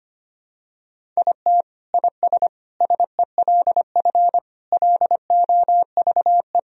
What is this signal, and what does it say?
Morse code, heard as a keyed tone, about 25 wpm, 700 Hz. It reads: IT IS SELF LOVE
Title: Sentences from Top 1000 Words - Review Part 1 - 25wpm